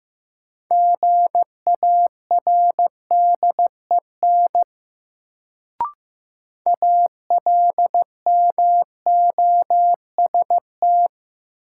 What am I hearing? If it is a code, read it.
Morse code, heard as a keyed tone, about 15 wpm, 700 Hz. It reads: GARDEN ALMOST
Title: Sentences from Top 700 Words - New Word Drill - 15wpm